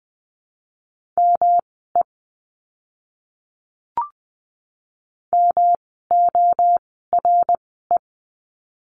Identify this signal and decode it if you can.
Morse code, heard as a keyed tone, about 20 wpm, spaced as for 14 wpm, 700 Hz. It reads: ME MORE
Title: 237 Lesson 38f - Letter q - Words 20wpm